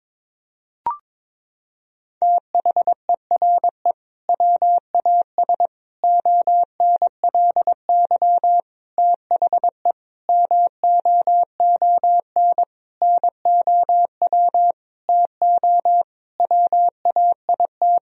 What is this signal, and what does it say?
Morse code, heard as a keyed tone, about 22 wpm, 700 Hz. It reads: THERE WAS ONLY THE MOON NOW TO WAIT
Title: Sentences from Top 1000 Words - Review Part 3 - 22wpm